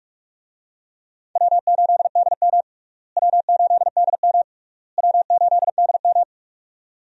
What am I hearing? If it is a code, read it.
Morse code, heard as a keyed tone, about 46 wpm, 700 Hz. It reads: W8DM W8DM W8DM
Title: USA Call Signs - 4 Characters - Repeated 3x - 45wpm